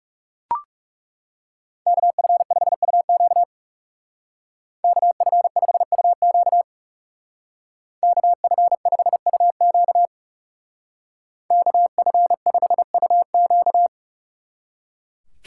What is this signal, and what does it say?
Morse code, heard as a keyed tone, about 36 wpm, 700 Hz. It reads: KF5UQ KF5UQ KF5UQ KF5UQ
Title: Speed Racing - USA 2x2 Call Signs 30wpm